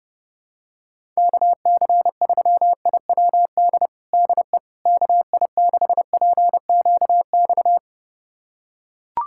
Morse code, heard as a keyed tone, about 30 wpm, 700 Hz. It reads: KC3IWD DE KI6PQX
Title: USA Callsign DE Callsign - 6 Characters - 30wpm